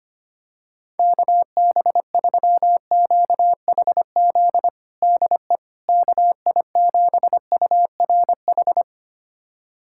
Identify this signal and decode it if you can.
Morse code, heard as a keyed tone, about 25 wpm, 700 Hz. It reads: KB3QHZ DE KI7URH
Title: USA Callsign DE Callsign - 6 Characters - 25wpm